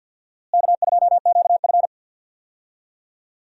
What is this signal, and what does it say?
Morse code, heard as a keyed tone, about 51 wpm, 700 Hz. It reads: K2QF
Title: USA Call Signs - 4 Characters - Repeated 2x - 50wpm